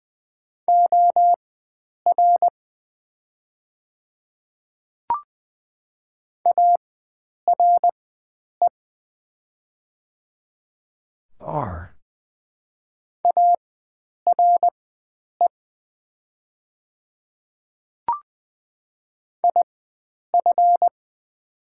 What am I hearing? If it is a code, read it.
Morse code, heard as a keyed tone, about 20 wpm, spaced as for 9 wpm, 700 Hz. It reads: OR ARE ARE IF
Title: 4x Character Spacing - Top 100 Words - 20wpm